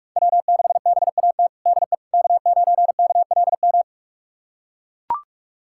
Morse code, heard as a keyed tone, about 44 wpm, 700 Hz. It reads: W6BAT DE K9KLM
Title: USA Callsign DE Callsign - 5 Characters - 45wpm